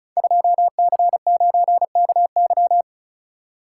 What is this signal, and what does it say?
Morse code, heard as a keyed tone, about 36 wpm, 700 Hz. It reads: 2C9KY
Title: Code Groups - 5 Letter:Numbers - 35wpm